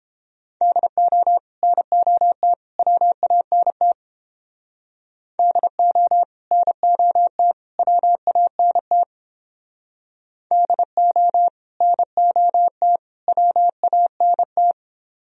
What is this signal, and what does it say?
Morse code, heard as a keyed tone, about 29 wpm, 700 Hz. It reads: DO NOT WANT DO NOT WANT DO NOT WANT